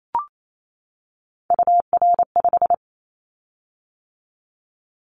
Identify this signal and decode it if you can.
Morse code, heard as a keyed tone, about 28 wpm, 700 Hz. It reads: UR5
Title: Callsign Prefixes for Top 25 DXCC - 28wpm